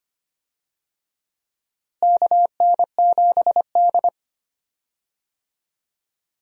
Morse code, heard as a keyed tone, about 25 wpm, 700 Hz. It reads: KN7D
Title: USA Call Signs - 4 Characters - Repeated 2x - 25wpm